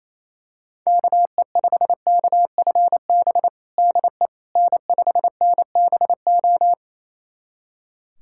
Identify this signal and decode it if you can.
Morse code, heard as a keyed tone, about 28 wpm, 700 Hz. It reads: KE5KFB DE N5NBO
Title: USA Callsign DE Callsign 28wpm